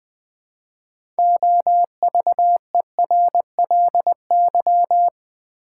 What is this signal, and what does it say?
Morse code, heard as a keyed tone, about 20 wpm, 700 Hz. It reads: OVERLY
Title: Build A Word - 2 Letters At A Time - 6-Letter Words - 20wpm